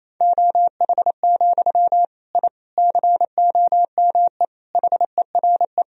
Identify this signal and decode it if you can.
Morse code, heard as a keyed tone, about 28 wpm, 700 Hz. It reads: OH, I COME HERE
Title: Sentences from Top 1000 Words - Review Part 2 - 28wpm